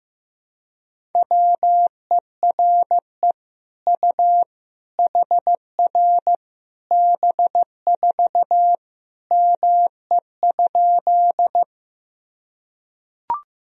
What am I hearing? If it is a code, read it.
Morse code, heard as a keyed tone, about 15 wpm, 700 Hz. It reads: WERE U HR B4 ME?
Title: Sentences from Top 200 Words and Abbreviations 15wpm